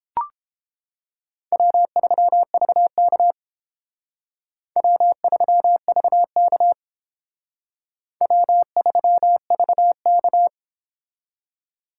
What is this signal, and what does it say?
Morse code, heard as a keyed tone, about 30 wpm, 700 Hz. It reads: W3VK W3VK W3VK